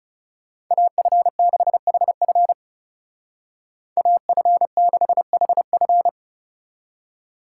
Morse code, heard as a keyed tone, about 39 wpm, 700 Hz. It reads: AF6HF AF6HF